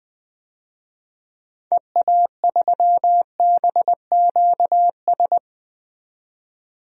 Morse code, heard as a keyed tone, about 20 wpm, 700 Hz. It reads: EA3BQS
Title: DXCC-100 Call Signs - 6 Characters - 20wpm